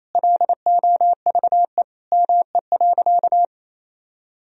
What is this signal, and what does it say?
Morse code, heard as a keyed tone, about 28 wpm, 700 Hz. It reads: LOVE ME.
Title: Sentences from Top 400 Words and Abbreviations 28wpm